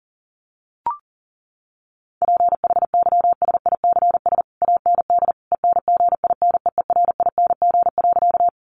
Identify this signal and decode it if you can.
Morse code, heard as a keyed tone, about 40 wpm, 700 Hz. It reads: PHYSICS AND ENGINEERING.
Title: Sentences from Top 1500 Words 40wpm